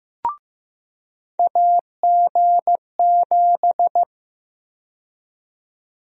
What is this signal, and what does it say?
Morse code, heard as a keyed tone, about 15 wpm, 700 Hz. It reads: AG7
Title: USA Call Sign Prefixes - 15wpm